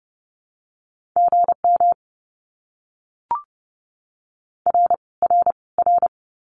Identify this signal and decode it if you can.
Morse code, heard as a keyed tone, about 30 wpm, 700 Hz. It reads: GM R R R
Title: Abbreviations Repeated 3x - 30wpm